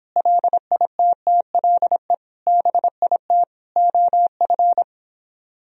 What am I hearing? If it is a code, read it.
Morse code, heard as a keyed tone, about 29 wpm, 700 Hz. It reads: LITTLE BIT OF